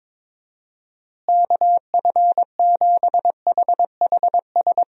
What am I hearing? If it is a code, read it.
Morse code, heard as a keyed tone, about 22 wpm, 700 Hz. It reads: KF7HHS
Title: USA Call Signs - 6 Characters - Repeated 2x - 22wpm